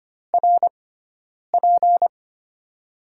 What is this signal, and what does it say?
Morse code, heard as a keyed tone, about 25 wpm, 700 Hz. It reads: R P
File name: Top 90 QSO Elements - 6x Character Spacing - 25wpm